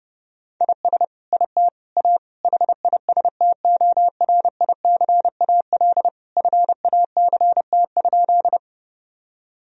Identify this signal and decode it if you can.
Morse code, heard as a keyed tone, about 30 wpm, 700 Hz. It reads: IS IT A HISTORICAL FACT?
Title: Sentences from Top 1000 Words - Review Part 4 - 30wpm